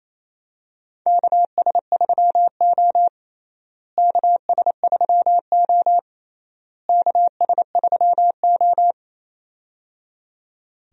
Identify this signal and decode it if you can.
Morse code, heard as a keyed tone, about 28 wpm, 700 Hz. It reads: KS3O KS3O KS3O
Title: USA Call Signs - 4 Characters - Repeated 3x - 28wpm